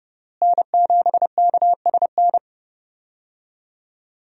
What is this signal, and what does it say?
Morse code, heard as a keyed tone, about 32 wpm, 700 Hz. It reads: N7KSN
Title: USA Call Signs - 5 Characters - Repeated 2x - 30wpm